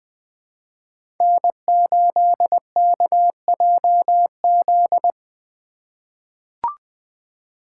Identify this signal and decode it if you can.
Morse code, heard as a keyed tone, about 20 wpm, 700 Hz. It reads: N8KJZ
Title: Code Groups - 5 Letter:Numbers - 20wpm